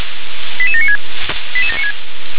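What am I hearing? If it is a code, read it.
Five-tone selective call (ZVEI), heard as quick runs of five tones, tones of about 80 ms. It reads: 88796
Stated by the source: ZVEI five-tone call received by SDR.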